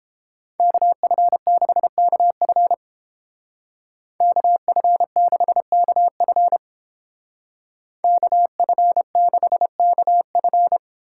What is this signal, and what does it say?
Morse code, heard as a keyed tone, about 29 wpm, 700 Hz. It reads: KF6KF KF6KF KF6KF